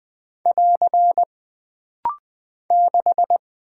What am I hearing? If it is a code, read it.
Morse code, heard as a keyed tone, about 20 wpm, 700 Hz. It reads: <AR> 6
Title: Single Character (Letter:Number:Punctuation:Prosign) - Rapid-Fire - 20wpm